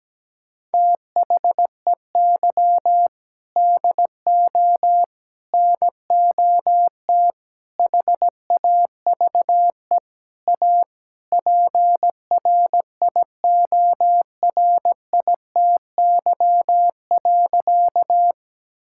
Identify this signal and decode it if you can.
Morse code, heard as a keyed tone, about 17 wpm, 700 Hz. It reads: THEY DO NOT HAVE A PRIORITY.
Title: Sentences from Top 1600 Words 17wpm